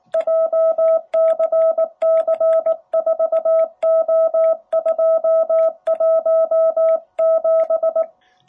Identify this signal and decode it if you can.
Morse code, heard as a keyed tone, about 19 wpm, 660 Hz. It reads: JCC4O217
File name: jh8cbh / 2021cw